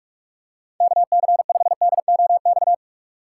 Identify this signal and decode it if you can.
Morse code, heard as a keyed tone, about 46 wpm, 700 Hz. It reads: KC5DOX
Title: USA Call Signs - 6 Characters - 3x Character Spacing - 45wpm